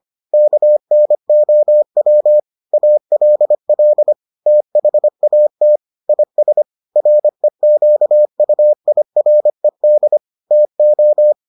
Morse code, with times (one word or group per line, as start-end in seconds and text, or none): 0.3-2.4 KNOW
2.7-4.1 ALL
4.5-5.8 THAT
6.1-6.6 IS
7.0-10.2 REQUIRED
10.5-11.3 TO